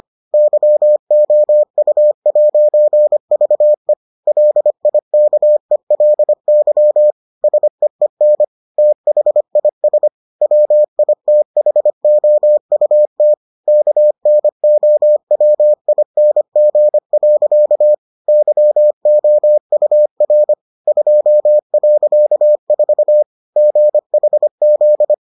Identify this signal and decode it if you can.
Morse code, heard as a keyed tone, about 25 wpm, 600 Hz. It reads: YOU'VE LIKELY SEEN THIS WITHOUT KNOWING. YOUR 2.4 GHZ